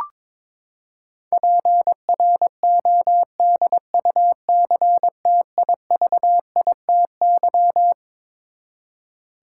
Morse code, heard as a keyed tone, about 22 wpm, 700 Hz. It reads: PRODUCTIVITY
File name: Twelve Letter Words - 22wpm